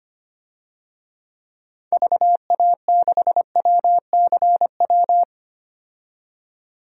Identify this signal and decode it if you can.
Morse code, heard as a keyed tone, about 25 wpm, 700 Hz. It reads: VA6WCW
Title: DXCC-100 Call Signs - 6 Characters - 25wpm